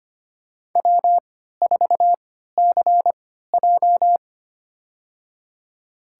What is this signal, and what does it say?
Morse code, heard as a keyed tone, about 25 wpm, spaced as for 14 wpm, 700 Hz. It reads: W4CJ
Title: USA Call Signs - 3x Character Spacing - 25wpm